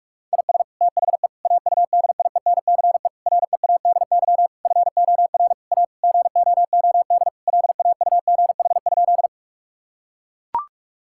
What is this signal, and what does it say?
Morse code, heard as a keyed tone, about 41 wpm, 700 Hz. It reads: IS THE AUDIENCE READY FOR A GOOD LAUGH?
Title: Sentences from Top 1000 Words - Review Part 5 - 45wpm